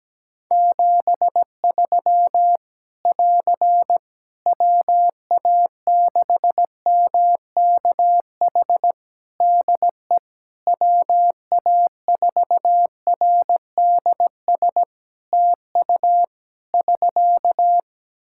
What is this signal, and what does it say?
Morse code, heard as a keyed tone, about 17 wpm, 700 Hz. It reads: 73 <AR> WA6MKH DE WA4RDS TU <SK>
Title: Practice QSO - Exchange 4 - Ending - 17wpm